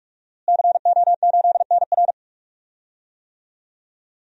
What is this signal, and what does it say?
Morse code, heard as a keyed tone, about 42 wpm, 700 Hz. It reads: CO8NR
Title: DXCC-100 Call Signs - 5 Characters - Repeated 3x - 45wpm